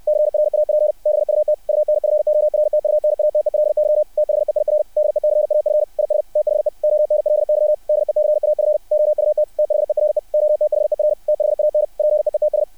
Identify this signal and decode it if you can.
Morse code, heard as a keyed tone, about 60 wpm, 600 Hz. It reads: CRAP BUT UNFORTUNATELY THEIR DECAF IS THE ONLY DECAF OUT THERE OTHER THAN PEETS
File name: archived / CRSnet